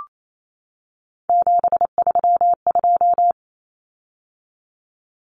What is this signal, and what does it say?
Morse code, heard as a keyed tone, about 28 wpm, 700 Hz. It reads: 732
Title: Common Cut Numbers - 3-Digit 28wpm